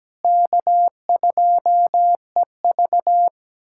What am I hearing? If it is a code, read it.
Morse code, heard as a keyed tone, about 17 wpm, 700 Hz. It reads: K2EV
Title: USA Callsign DE Callsign - 4 Characters - 17wpm